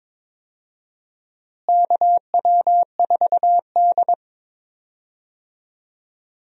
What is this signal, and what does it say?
Morse code, heard as a keyed tone, about 22 wpm, 700 Hz. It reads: KW4D
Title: USA Call Signs - 4 Characters - Repeated 3x - 22wpm